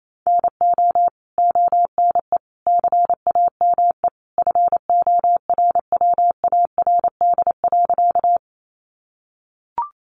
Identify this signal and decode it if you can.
Morse code, heard as a keyed tone, about 28 wpm, 700 Hz. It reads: NO ONE CAME FORWARD.
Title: Sentences from Top 1300 Words 28wpm